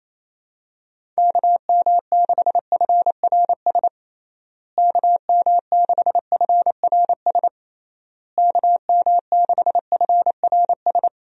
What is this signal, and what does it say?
Morse code, heard as a keyed tone, about 27 wpm, 700 Hz. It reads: KM6FRS KM6FRS KM6FRS
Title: USA Call Signs - 6 Characters - Repeated 3x - 28wpm